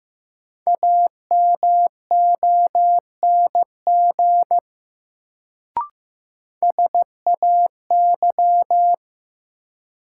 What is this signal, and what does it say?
Morse code, heard as a keyed tone, about 16 wpm, 700 Hz. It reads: AMONG SAY
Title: Sets of 2 Words - New Word Drill - 15wpm